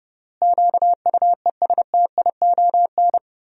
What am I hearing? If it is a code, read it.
Morse code, heard as a keyed tone, about 31 wpm, 700 Hz. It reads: QUESTION
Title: 3x Word-Spacing - Sentences from Top 400 Words - 30wpm